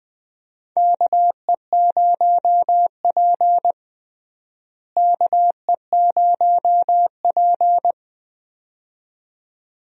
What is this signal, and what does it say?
Morse code, heard as a keyed tone, about 21 wpm, 700 Hz. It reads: KE0P KE0P
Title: USA Call Signs - 4 Characters - Repeated 2x - 20wpm